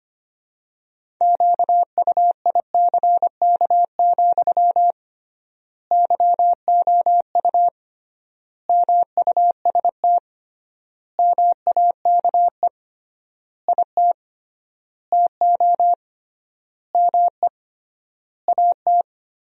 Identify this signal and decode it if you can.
Morse code, heard as a keyed tone, about 25 wpm, 700 Hz. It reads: QUICK, YOU MUST MAKE IT TO ME AT
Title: 3x Word-Spacing - Sentences from Top 500 Words - 25wpm